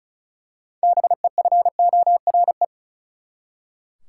Morse code, heard as a keyed tone, about 34 wpm, 700 Hz. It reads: BEFORE
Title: Six Letter Words - 35wpm